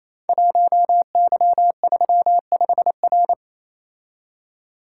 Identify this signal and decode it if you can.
Morse code, heard as a keyed tone, about 28 wpm, 700 Hz. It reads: 1Y35R
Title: Code Groups - 5 Letter:Numbers - 28wpm